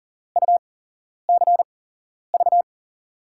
Morse code, heard as a keyed tone, about 40 wpm, spaced as for 11 wpm, 700 Hz. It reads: UCV
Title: USA Call Signs - 8x Character Spacing - 40wpm